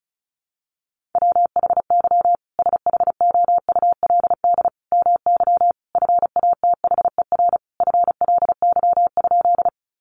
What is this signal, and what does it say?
Morse code, heard as a keyed tone, about 34 wpm, 700 Hz. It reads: WHY SHOULD MY FATHER FLY?
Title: Sentences from Top 500 Words 35wpm